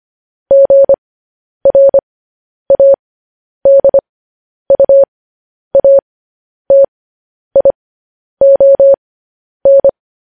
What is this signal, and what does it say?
Morse code, heard as a keyed tone, about 25 wpm, spaced as for 10 wpm, 550 Hz. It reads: GRADUATION